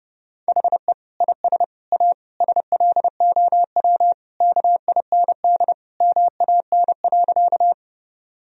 Morse code, heard as a keyed tone, about 30 wpm, 700 Hz. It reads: HE IS A SLOW KIND MAN.
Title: Sentences from Top 1000 Words - Review Part 2 - 30wpm